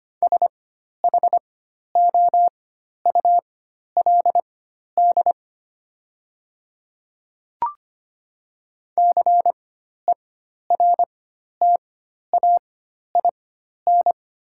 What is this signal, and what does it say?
Morse code, heard as a keyed tone, about 25 wpm, spaced as for 12 wpm, 700 Hz. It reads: SHOULD CERTAIN
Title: 4x Character Spacing - Top 500 Words - 25wpm